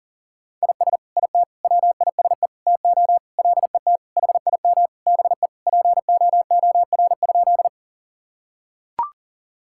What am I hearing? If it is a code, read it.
Morse code, heard as a keyed tone, about 40 wpm, 700 Hz. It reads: IS IT WISE TO LET HIM BE POOR?
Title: Sentences from Top 1600 Words 40wpm